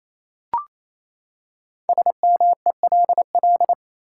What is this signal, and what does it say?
Morse code, heard as a keyed tone, about 28 wpm, 700 Hz. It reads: SMELL
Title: Five Letter Words - 28wpm